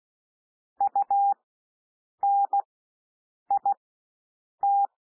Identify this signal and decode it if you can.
Morse code, heard as a keyed tone, about 16 wpm, spaced as for 7 wpm, 800 Hz. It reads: UNIT
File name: cwmp3 / W1 SOUND WORDS 16 wpm